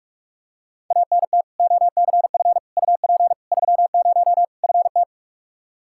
Morse code, heard as a keyed tone, about 46 wpm, 700 Hz. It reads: ANT OCF UP 30 FT
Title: Practice QSO - Exchange 2 - Info - 45wpm